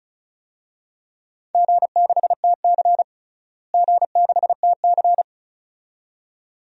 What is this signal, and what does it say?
Morse code, heard as a keyed tone, about 36 wpm, 700 Hz. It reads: G6TC G6TC
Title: DXCC-100 Call Signs - 4 Characters - Repeated 2x - 35wpm